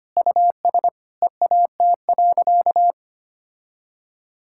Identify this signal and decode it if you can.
Morse code, heard as a keyed tone, about 25 wpm, 700 Hz. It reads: US EAT.
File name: practice / 25wpm / Sentences with 3-Word Phrases - 25wpm